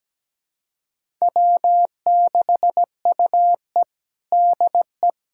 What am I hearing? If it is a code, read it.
Morse code, heard as a keyed tone, about 17 wpm, 700 Hz. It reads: W6UE DE
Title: USA Callsign DE Callsign - 4 Characters - 17wpm